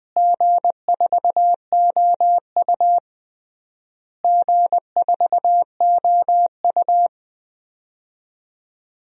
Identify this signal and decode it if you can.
Morse code, heard as a keyed tone, about 20 wpm, 700 Hz. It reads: G4OU G4OU
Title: DXCC-100 Call Signs - 4 Characters - Repeated 3x - 20wpm